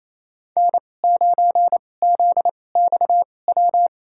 Morse code, heard as a keyed tone, about 28 wpm, spaced as for 20 wpm, 700 Hz. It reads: N9ZXW
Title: USA Call Signs - 5 Characters - 2x Character Spacing - 28wpm